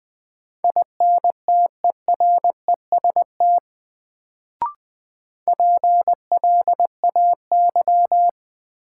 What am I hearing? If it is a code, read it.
Morse code, heard as a keyed tone, about 20 wpm, 700 Hz. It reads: INTEREST PLAY
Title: Sets of 5 Words Encore - New Word Drill - 20wpm